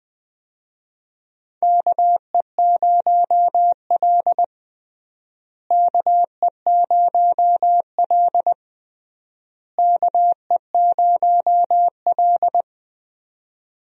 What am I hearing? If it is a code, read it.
Morse code, heard as a keyed tone, about 20 wpm, 700 Hz. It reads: KE0L KE0L KE0L
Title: USA Call Signs - 4 Characters - Repeated 3x - 20wpm